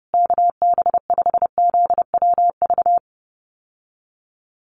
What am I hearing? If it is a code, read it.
Morse code, heard as a keyed tone, about 30 wpm, 700 Hz. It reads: KB5ZWV
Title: USA Call Signs Repeated 3x 30wpm